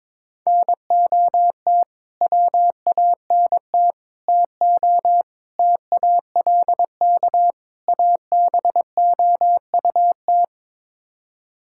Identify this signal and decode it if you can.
Morse code, heard as a keyed tone, about 22 wpm, 700 Hz. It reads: NOT WANT TO TALK ABOUT
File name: practice / 22wpm / Build A Phrase - 5-Word Phrases - 22wpm